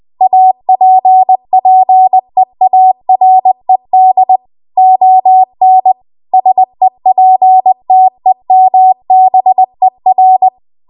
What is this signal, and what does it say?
Morse code, heard as a keyed tone, about 20 wpm, 750 Hz. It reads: APPEARED ON SEPTEMBER
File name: Archive / 20 WPM / 20WPM